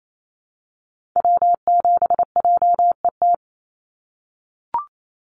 Morse code, heard as a keyed tone, about 28 wpm, 700 Hz. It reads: W7JET
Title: NA SOTA Activator Callsigns 28wpm